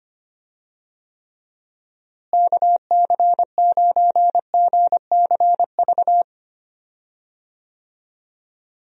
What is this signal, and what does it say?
Morse code, heard as a keyed tone, about 25 wpm, 700 Hz. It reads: KC9GCV
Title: USA Call Signs - 6 Characters - Repeated 2x - 25wpm